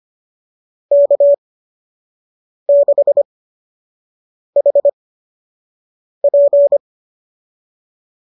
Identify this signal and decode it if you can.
Morse code, heard as a keyed tone, about 25 wpm, spaced as for 6 wpm, 590 Hz. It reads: K6HP
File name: qso003 06